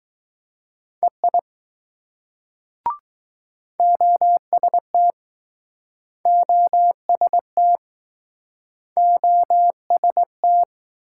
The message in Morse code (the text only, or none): EI OST OST OST